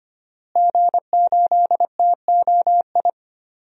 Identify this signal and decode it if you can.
Morse code, heard as a keyed tone, about 25 wpm, 700 Hz. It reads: G8TOI
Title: DXCC-100 Call Signs - 5 Characters - Repeated 3x - 25wpm